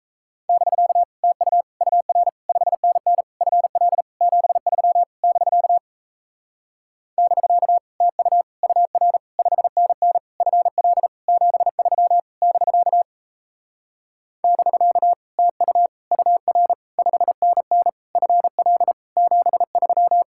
Morse code, heard as a keyed tone, about 37 wpm, 700 Hz. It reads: <BK> TU UR 5NN FL 73 <BK> <BK> TU UR 5NN FL 73 <BK> <BK> TU UR 5NN FL 73